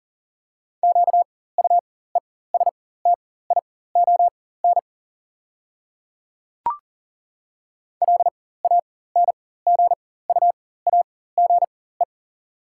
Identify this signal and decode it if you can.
Morse code, heard as a keyed tone, about 40 wpm, spaced as for 19 wpm, 700 Hz. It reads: QUESTION LANGUAGE
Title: 4x Character Spacing - Top 500 Words - 40wpm